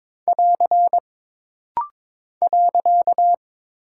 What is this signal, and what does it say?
Morse code, heard as a keyed tone, about 22 wpm, 700 Hz. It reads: <AR> .
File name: practice / 22wpm / Single Punctuation-Prosign - Rapid-Fire - 22wpm